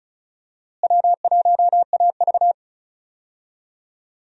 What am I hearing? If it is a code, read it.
Morse code, heard as a keyed tone, about 34 wpm, 700 Hz. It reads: W1AV
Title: USA Call Signs - 4 Characters - Repeated 2x - 35wpm